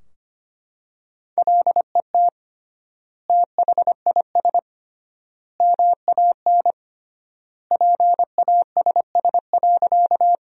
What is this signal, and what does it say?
Morse code, heard as a keyed tone, about 25 wpm, 700 Hz. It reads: LET THIS MAN PASS.
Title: 3x Word-Spacing - Sentences from Top 400 Words - 25wpm